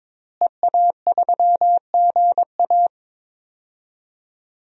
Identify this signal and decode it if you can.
Morse code, heard as a keyed tone, about 22 wpm, 700 Hz. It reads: EA3GA
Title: DXCC-100 Call Signs - 5 Characters - Repeated 3x - 22wpm